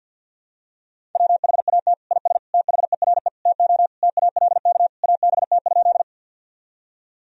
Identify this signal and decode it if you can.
Morse code, heard as a keyed tone, about 50 wpm, 700 Hz. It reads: WHAT IS THERE TO TALK ABT?
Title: Sentences from Top 400 Words and Abbreviations 50wpm